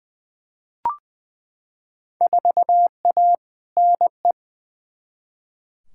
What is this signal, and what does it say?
Morse code, heard as a keyed tone, about 21 wpm, 700 Hz. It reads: 4A NE
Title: Field Day Exchanges - 20wpm